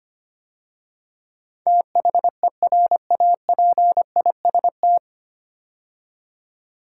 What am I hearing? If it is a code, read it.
Morse code, heard as a keyed tone, about 26 wpm, 700 Hz. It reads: THERAPIST
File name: practice / 25wpm / 9 Letter Words - First 3 Letters Given - 25wpm